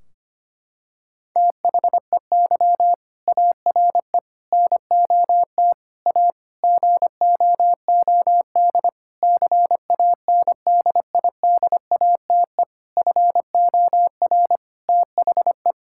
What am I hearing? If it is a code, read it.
Morse code, heard as a keyed tone, about 25 wpm, 700 Hz. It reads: THEY ARE NOT A GOOD CANDIDATE FOR THE
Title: Sentences from Top 1600 Words 25wpm